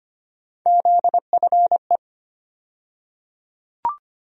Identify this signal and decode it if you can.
Morse code, heard as a keyed tone, about 25 wpm, 700 Hz. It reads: ZFE
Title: USA Call Sign Postfixes - 3 Letters 25wpm